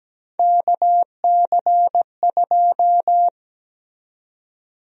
Morse code, heard as a keyed tone, about 17 wpm, 700 Hz. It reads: KC2
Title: USA Call Sign Prefixes - 17wpm